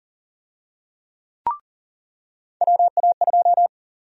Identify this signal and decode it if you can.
Morse code, heard as a keyed tone, about 40 wpm, 700 Hz. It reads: WA2
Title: USA Call Sign Prefixes - 40wpm